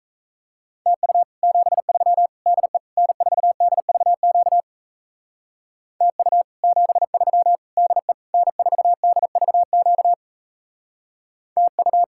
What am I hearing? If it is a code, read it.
Morse code, heard as a keyed tone, about 36 wpm, 700 Hz. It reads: TU 73 DE N4DVQ TU 73 DE N4DVQ TU